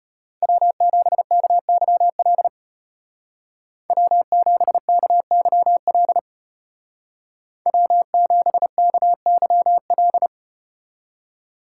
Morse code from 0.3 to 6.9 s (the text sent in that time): W7KYL W7KYL